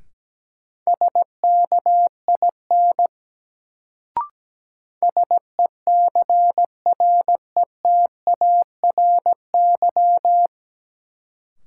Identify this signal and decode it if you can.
Morse code, heard as a keyed tone, about 17 wpm, 700 Hz. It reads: SKIN SECRETARY
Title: Sentences from Top 1300 Words - New Word Drill - 17wpm